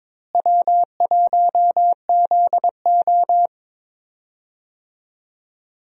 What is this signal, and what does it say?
Morse code, heard as a keyed tone, about 22 wpm, 700 Hz. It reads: W1ZO
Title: USA Call Signs - 4 Characters - Repeated 2x - 22wpm